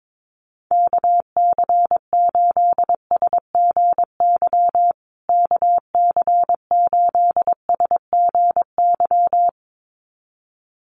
Morse code, heard as a keyed tone, about 22 wpm, 700 Hz. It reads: KC8SGY KC8SGY
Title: USA Call Signs Repeated 2x 22wpm